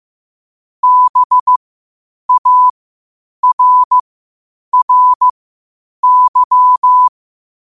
Morse code, heard as a keyed tone, about 15 wpm, spaced as for 8 wpm, 1000 Hz. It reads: BARRY
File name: qotdfiles / 8wpm